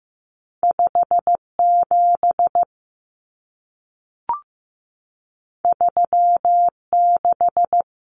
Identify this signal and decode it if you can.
Morse code, heard as a keyed tone, about 15 wpm, 700 Hz. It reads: 57 36
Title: Common Cut Numbers - 2-Digit 15wpm